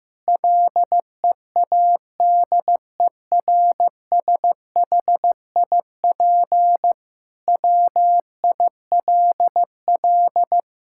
Morse code, heard as a keyed tone, about 15 wpm, 700 Hz. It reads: LEADERSHIP WILL
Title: Sentences from Top 1600 Words 15wpm